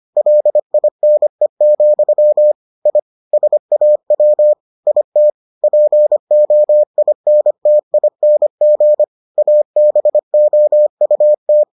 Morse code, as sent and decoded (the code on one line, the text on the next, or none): .-.. .. -. --..-- / .. / ... .- .-- / .. - / .--. --- .. -. - .. -. --. / .- -... --- ..- -
LINE, I SAW IT POINTING ABOUT